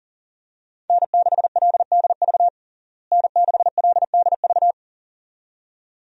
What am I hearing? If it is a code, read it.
Morse code, heard as a keyed tone, about 40 wpm, 700 Hz. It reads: N6LDV N6LDV
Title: USA Call Signs - 5 Characters - Repeated 2x - 40wpm